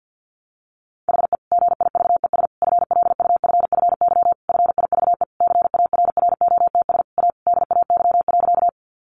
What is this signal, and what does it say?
Morse code, heard as a keyed tone, about 50 wpm, 700 Hz. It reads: HE GIVES FLUFFY FIVE CARROTS A DAY.
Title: Easy Phrases 50wpm